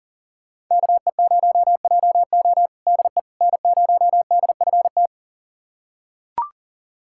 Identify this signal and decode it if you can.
Morse code, heard as a keyed tone, about 40 wpm, 700 Hz. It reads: KE0JO DE N0DFT
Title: USA Callsign DE Callsign - 5 Characters - 40wpm